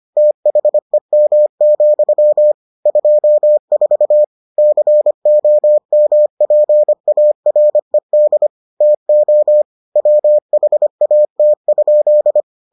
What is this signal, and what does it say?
Morse code, heard as a keyed tone, about 25 wpm, 600 Hz. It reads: THEM, 24 COMPARED TO WHAT?